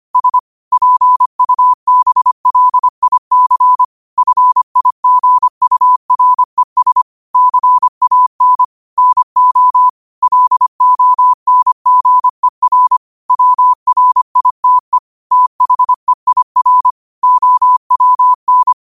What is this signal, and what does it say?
Morse code, heard as a keyed tone, about 25 wpm, 1000 Hz. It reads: S PUBLIC FIGURES CAN NO LONGER WRITE THEIR OWN
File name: qotdfiles / 25wpm